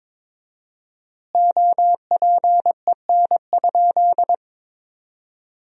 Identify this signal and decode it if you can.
Morse code, heard as a keyed tone, about 22 wpm, 700 Hz. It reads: OPEN?
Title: 4x Word-Spacing - Sentences from Top 300 Words - 22wpm